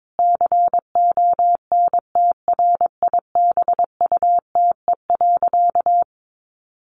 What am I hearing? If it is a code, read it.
Morse code, heard as a keyed tone, about 22 wpm, 700 Hz. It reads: CONTRIBUTE.
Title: Sentences from Top 1400 Words 22wpm